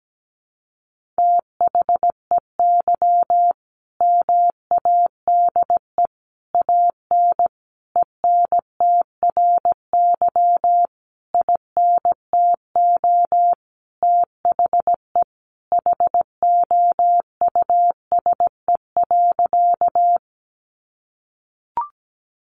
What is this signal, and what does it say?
Morse code, heard as a keyed tone, about 17 wpm, 700 Hz. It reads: THEY MADE AN ENTRY INTO THE HOUSE.
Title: Sentences from Top 1500 Words 17wpm